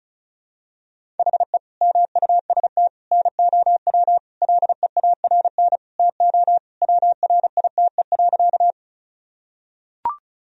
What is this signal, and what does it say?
Morse code, heard as a keyed tone, about 34 wpm, 700 Hz. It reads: HE MUST NOW LEARN TO WRITE.
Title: Sentences from Top 1000 Words - Review Part 1 - 35wpm